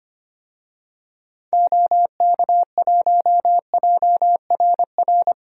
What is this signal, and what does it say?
Morse code, heard as a keyed tone, about 25 wpm, 700 Hz. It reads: OK1JRR
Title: DXCC-100 Call Signs - 6 Characters - Repeated 3x - 25wpm